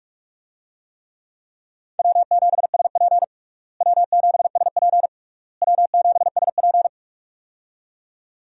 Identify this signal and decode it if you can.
Morse code, heard as a keyed tone, about 41 wpm, 700 Hz. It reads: W7SP W7SP W7SP
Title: USA Call Signs - 4 Characters - Repeated 3x - 45wpm